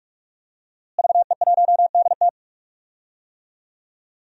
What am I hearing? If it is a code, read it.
Morse code, heard as a keyed tone, about 45 wpm, 700 Hz. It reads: VE1DT
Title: DXCC-100 Call Signs - 5 Characters - Repeated 3x - 45wpm